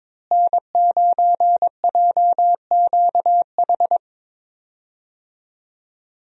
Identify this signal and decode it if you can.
Morse code, heard as a keyed tone, about 22 wpm, 700 Hz. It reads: N9JQH
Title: USA Call Signs - 5 Characters - Repeated 2x - 22wpm